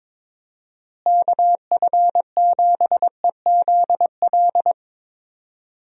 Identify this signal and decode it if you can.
Morse code, heard as a keyed tone, about 22 wpm, 700 Hz. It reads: KF7EZL
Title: USA Call Signs - 6 Characters - Repeated 2x - 22wpm